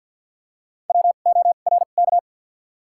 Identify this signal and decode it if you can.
Morse code, heard as a keyed tone, about 47 wpm, 700 Hz. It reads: W O R K
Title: 2x Character Spacing - Top 500 Words - 50wpm